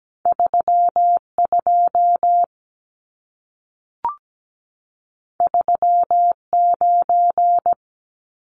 Morse code, heard as a keyed tone, about 17 wpm, 700 Hz. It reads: 32 39
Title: Numbers - 2-Digit 17wpm